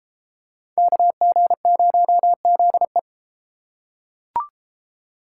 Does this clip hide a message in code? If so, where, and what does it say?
Yes: Morse code from 0.8 to 3.0 s, heard as a keyed tone, about 32 wpm, 700 Hz. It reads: KG0ZE